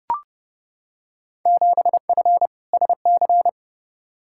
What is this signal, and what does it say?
Morse code, heard as a keyed tone, about 32 wpm, 700 Hz. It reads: 7F SC
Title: Field Day Exchanges Encore - 30wpm